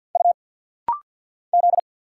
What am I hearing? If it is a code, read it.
Morse code, heard as a keyed tone, about 50 wpm, 700 Hz. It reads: U Z
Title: Single Letters - Rapid-Fire 50wpm